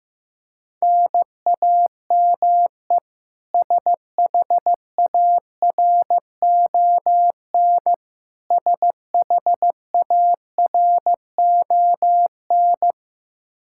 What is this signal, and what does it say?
Morse code, heard as a keyed tone, about 15 wpm, 700 Hz. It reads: NAME SHARON SHARON
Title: K1USN SST QSO Practice 15wpm